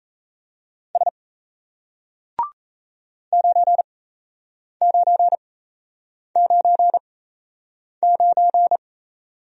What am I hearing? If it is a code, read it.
Morse code, heard as a keyed tone, about 37 wpm, spaced as for 8 wpm, 700 Hz. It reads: S 9999